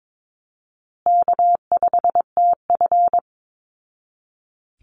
Morse code, heard as a keyed tone, about 22 wpm, 700 Hz. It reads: K5TF
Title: Call Signs Repeated 2x - 22wpm